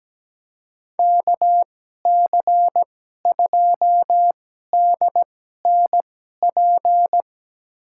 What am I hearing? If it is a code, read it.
Morse code, heard as a keyed tone, about 17 wpm, spaced as for 12 wpm, 700 Hz. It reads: KC2DNP
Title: USA Call Signs - 6 Characters - 2x Character Spacing - 17wpm